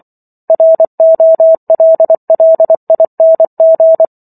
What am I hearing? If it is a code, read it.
Morse code, heard as a keyed tone, about 24 wpm, 650 Hz. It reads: ROLLING